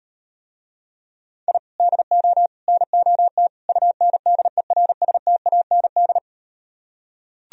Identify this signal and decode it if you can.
Morse code, heard as a keyed tone, about 38 wpm, 700 Hz. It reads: I DO NOT UNDERSTAND